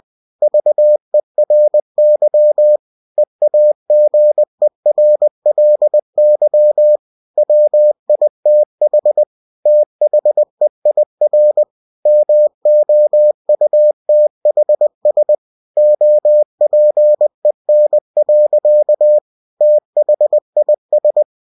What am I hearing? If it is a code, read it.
Morse code, heard as a keyed tone, about 20 wpm, 600 Hz. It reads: VERY EAGERLY WITH THEIR MOUTHS OPEN. THIS
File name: cw-books / winnie-the-pooh / winnie-20wpm-page-0063